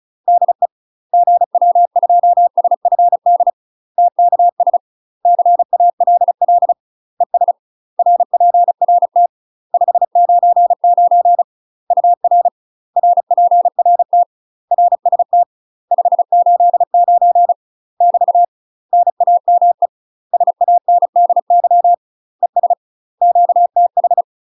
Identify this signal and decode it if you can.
Morse code, heard as a keyed tone, about 35 wpm, 700 Hz. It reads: DE GW2SFD TKS CALL ES RPRT 599 UR RPRT RST 589 <BT> NAME SANDY ES QTH
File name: UK MASTER QSOs TRAINING / 35WPM / UK MASTER QSOs 0016